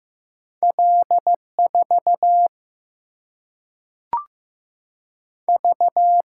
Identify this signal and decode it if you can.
Morse code, heard as a keyed tone, about 15 wpm, 700 Hz. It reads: L4 V
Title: Single-Letter Single-Number 15wpm